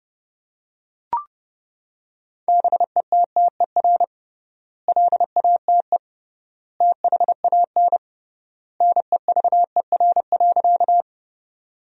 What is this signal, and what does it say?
Morse code, heard as a keyed tone, about 32 wpm, 700 Hz. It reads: BETTER LATE THAN NEVER.
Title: 3x Word-Spacing - Sentences from Top 400 Words - 30wpm